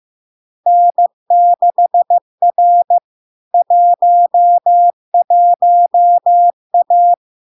Morse code, heard as a keyed tone, about 15 wpm, 700 Hz. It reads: N6R 11A
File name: LARK-FD-2022-Log-in-CW-15wpm-700-Hz